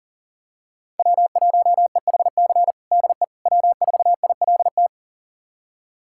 Morse code, heard as a keyed tone, about 40 wpm, 700 Hz. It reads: W1EHC DE W4ILT
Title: USA Callsign DE Callsign - 5 Characters - 40wpm